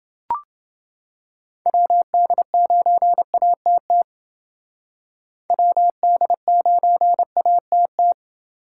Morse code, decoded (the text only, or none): WD9ATT WD9ATT